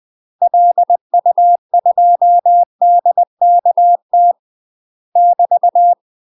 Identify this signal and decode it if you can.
Morse code, heard as a keyed tone, about 20 wpm, 700 Hz. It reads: LU2DKT <BT>
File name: UK MASTER QSOs TRAINING / 20WPM / UK MASTER QSOs 0032